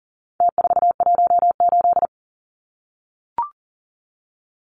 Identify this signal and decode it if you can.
Morse code, heard as a keyed tone, about 40 wpm, 700 Hz. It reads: T418
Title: Numbers - Contest Style - 4-Digit 40wpm